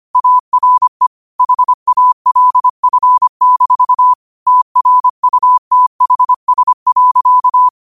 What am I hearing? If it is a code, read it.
Morse code, heard as a keyed tone, about 25 wpm, 1000 Hz. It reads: ARE HALF-TRUTHS.
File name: qotdfiles / 25wpm